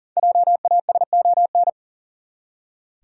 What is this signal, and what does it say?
Morse code, heard as a keyed tone, about 37 wpm, 700 Hz. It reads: JASON
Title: USA Common Names - Max 5 Letters - 40wpm